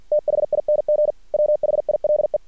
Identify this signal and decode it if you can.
Morse code, heard as a keyed tone, about 50 wpm, 600 Hz. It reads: THING WHILE
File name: archived / CRSnet